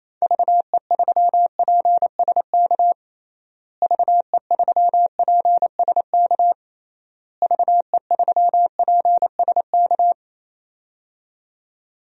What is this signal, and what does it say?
Morse code, heard as a keyed tone, about 27 wpm, 700 Hz. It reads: VE3PSK VE3PSK VE3PSK
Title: DXCC-100 Call Signs - 6 Characters - Repeated 3x - 28wpm